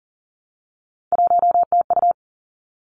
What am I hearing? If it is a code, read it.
Morse code, heard as a keyed tone, about 39 wpm, 700 Hz. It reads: 1TU
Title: Three Challenging Letter-Number Combinations 40wpm